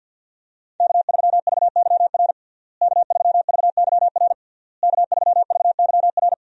Morse code, heard as a keyed tone, about 51 wpm, 700 Hz. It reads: K3VYR K3VYR K3VYR
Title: USA Call Signs - 5 Characters - Repeated 3x - 50wpm